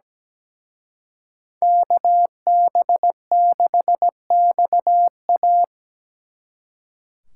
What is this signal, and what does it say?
Morse code, heard as a keyed tone, about 20 wpm, 700 Hz. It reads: KB6XA
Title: Speed Racing - USA 2x2 Call Signs 17wpm